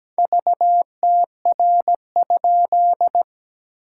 Morse code, heard as a keyed tone, about 17 wpm, 700 Hz. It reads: VTR?
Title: SOTA QSO Practice 17wpm